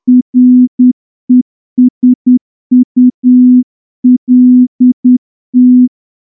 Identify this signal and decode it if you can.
Morse code, heard as a keyed tone, about 10 wpm, 250 Hz. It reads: RESULT
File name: morse-training-10WPM